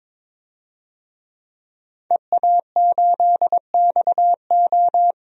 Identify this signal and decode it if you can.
Morse code, heard as a keyed tone, about 22 wpm, 700 Hz. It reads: EA8XO
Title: DXCC-100 Call Signs - 5 Characters - Repeated 3x - 22wpm